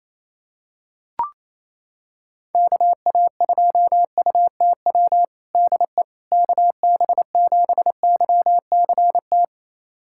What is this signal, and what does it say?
Morse code, heard as a keyed tone, about 28 wpm, 700 Hz. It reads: KA2UTW DE KB7YCT
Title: USA Callsign DE Callsign 28wpm